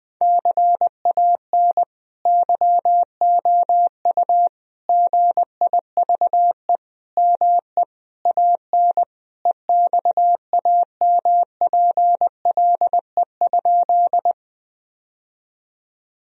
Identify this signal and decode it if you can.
Morse code, heard as a keyed tone, about 20 wpm, 700 Hz. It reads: CAN YOU GIVE ME AN EXAMPLE?
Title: Sentences from Top 1000 Words - Review Part 1 - 20wpm